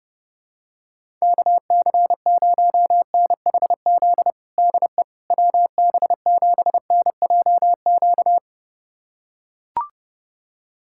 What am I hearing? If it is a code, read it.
Morse code, heard as a keyed tone, about 30 wpm, 700 Hz. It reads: KC0NHZ DE WB7NJQ
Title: USA Callsign DE Callsign - 6 Characters - 30wpm